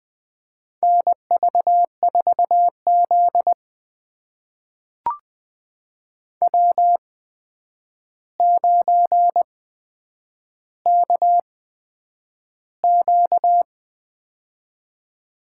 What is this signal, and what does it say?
Morse code, heard as a keyed tone, about 20 wpm, spaced as for 9 wpm, 700 Hz. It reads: NV4Z W9KQ
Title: USA Call Signs - 4 Characters - 8x Character Spacing - 20wpm